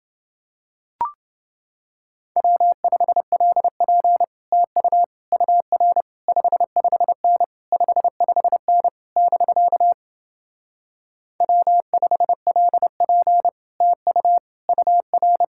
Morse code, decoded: W5LP TU UR 55N 55N <BK> W5LP TU UR

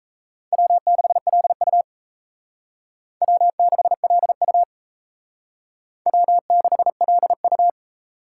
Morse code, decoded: W6LU W6LU W6LU

